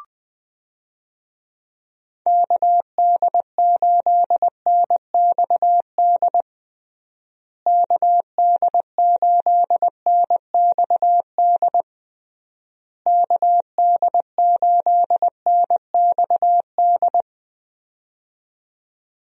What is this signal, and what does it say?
Morse code, heard as a keyed tone, about 21 wpm, 700 Hz. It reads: KD8NXD KD8NXD KD8NXD
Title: USA Call Signs - 6 Characters - Repeated 3x - 20wpm